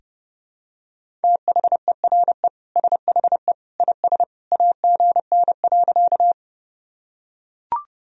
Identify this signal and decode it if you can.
Morse code, heard as a keyed tone, about 30 wpm, 700 Hz. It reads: THERE SHE IS AGN.
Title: Sentences from Top 200 Words and Abbreviations 30wpm